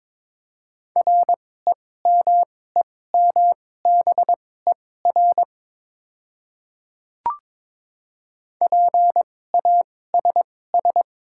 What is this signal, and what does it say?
Morse code, heard as a keyed tone, about 22 wpm, spaced as for 16 wpm, 700 Hz. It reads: REMEMBER PASS
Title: 2x Character Spacing - Top 500 Words - 22wpm